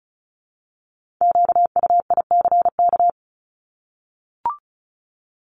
Morse code, heard as a keyed tone, about 36 wpm, 700 Hz. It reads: QUICK
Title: Multi-letter Phonemes Repeated 3x 35wpm